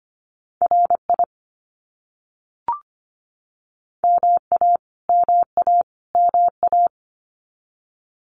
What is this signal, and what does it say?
Morse code, heard as a keyed tone, about 25 wpm, 700 Hz. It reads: RI MA MA MA
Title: US States Repeated 3x 25wpm